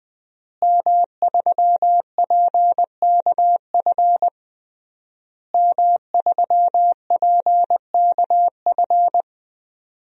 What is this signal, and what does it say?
Morse code, heard as a keyed tone, about 20 wpm, 700 Hz. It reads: M3PKF M3PKF
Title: DXCC-100 Call Signs - 5 Characters - Repeated 3x - 20wpm